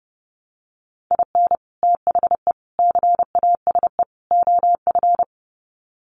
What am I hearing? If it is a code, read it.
Morse code, heard as a keyed tone, about 28 wpm, 700 Hz. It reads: IN THE CASE OF
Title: Sets of 4 Words 30wpm